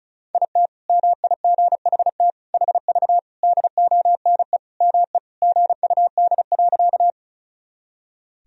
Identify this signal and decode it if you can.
Morse code, heard as a keyed tone, about 36 wpm, 700 Hz. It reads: IT MIGHT HV DONE ME GUD.
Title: Sentences from Top 500 Words and Abbreviations 35wpm